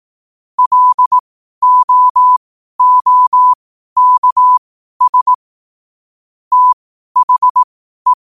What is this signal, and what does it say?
Morse code, heard as a keyed tone, about 18 wpm, spaced as for 12 wpm, 1000 Hz. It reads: LOOKS THE